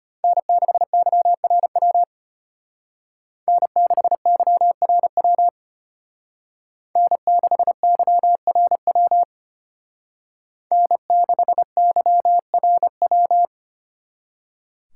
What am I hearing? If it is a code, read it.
Morse code, heard as a keyed tone, about 32 wpm, 700 Hz. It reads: N6YRW N6YRW N6YRW N6YRW